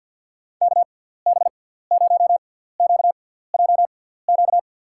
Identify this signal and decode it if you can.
Morse code, heard as a keyed tone, about 52 wpm, spaced as for 17 wpm, 700 Hz. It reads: KB0QJQ
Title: USA Call Signs - 6 Characters - 6x Character Spacing - 50wpm